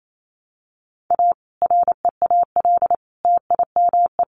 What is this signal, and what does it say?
Morse code, heard as a keyed tone, about 28 wpm, 700 Hz. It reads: A REAL TIME
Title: A Sound Of Thunder 28wpm